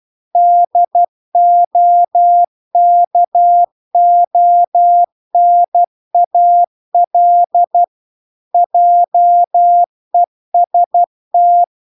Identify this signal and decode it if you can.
Morse code, heard as a keyed tone, about 12 wpm, 700 Hz. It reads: DOKONAL JEST